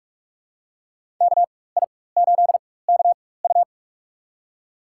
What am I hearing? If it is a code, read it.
Morse code, heard as a keyed tone, about 46 wpm, spaced as for 21 wpm, 700 Hz. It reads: KI8KU
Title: USA Call Signs - 5 Characters - 4x Character Spacing - 45wpm